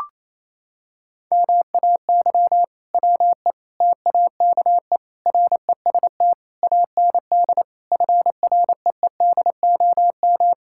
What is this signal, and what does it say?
Morse code, heard as a keyed tone, about 28 wpm, 700 Hz. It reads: MAY WE TAKE REST AND FREEDOM
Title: Sentences from Top 1600 Words 28wpm